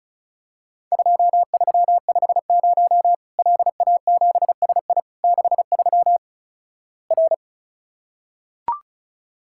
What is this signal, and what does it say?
Morse code, heard as a keyed tone, about 36 wpm, 700 Hz. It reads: 2350 LA7SI 63 R
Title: Worked All Europe DX Contest - QTC Exchange - 35wpm